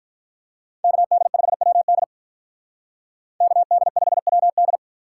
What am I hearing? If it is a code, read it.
Morse code, heard as a keyed tone, about 46 wpm, 700 Hz. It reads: KD5WD KD5WD